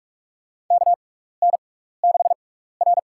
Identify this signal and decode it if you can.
Morse code, heard as a keyed tone, about 46 wpm, spaced as for 16 wpm, 700 Hz. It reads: KN6R
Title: USA Call Signs - 4 Characters - 6x Character Spacing - 45wpm